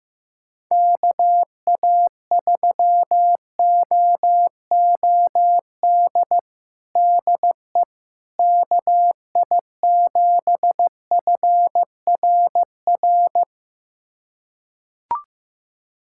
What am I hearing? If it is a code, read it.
Morse code, heard as a keyed tone, about 16 wpm, 700 Hz. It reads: KA3OOD DE KI7FRR
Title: USA Callsign DE Callsign 15wpm